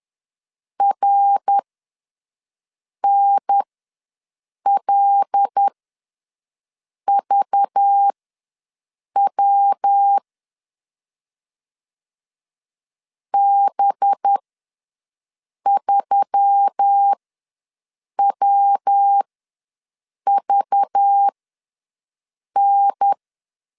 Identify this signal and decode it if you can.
Morse code, heard as a keyed tone, about 10 wpm, spaced as for 5 wpm, 800 Hz. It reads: RNLVW B3WVN
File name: LEZIONI / 5 LBVC / 5 LBV ws 10wpm 3645